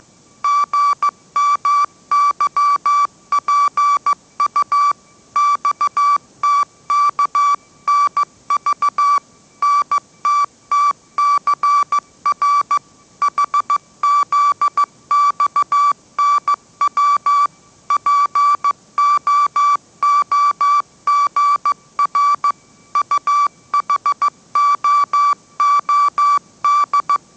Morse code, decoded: GMYPU XTKNV NTTCR HZXNW POOGR UHOOD

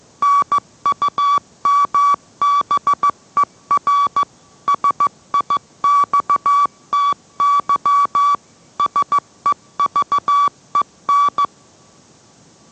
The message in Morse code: NUMBER SIXTY SEVEN

